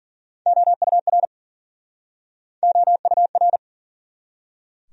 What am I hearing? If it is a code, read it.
Morse code, heard as a keyed tone, about 52 wpm, 700 Hz. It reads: OUR OUR